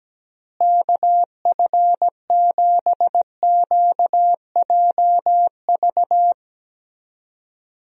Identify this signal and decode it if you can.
Morse code, heard as a keyed tone, about 17 wpm, 700 Hz. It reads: KF7QJV
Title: USA Call Signs - 6 Characters - Repeated 3x - 17wpm